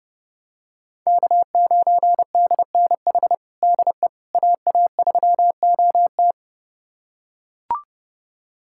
Morse code, heard as a keyed tone, about 30 wpm, 700 Hz. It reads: K9DNH DE AA3OT
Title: USA Callsign DE Callsign - 5 Characters - 30wpm